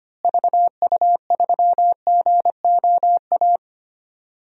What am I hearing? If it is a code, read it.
Morse code, heard as a keyed tone, about 25 wpm, 700 Hz. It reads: VU3GOA
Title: DXCC-100 Call Signs - 6 Characters - Repeated 2x - 25wpm